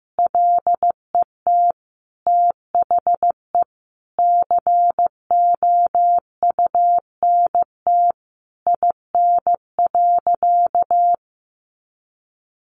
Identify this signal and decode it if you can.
Morse code, heard as a keyed tone, about 15 wpm, 700 Hz. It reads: LET THE COUNT IN.
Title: Sentences from Top 1500 Words 15wpm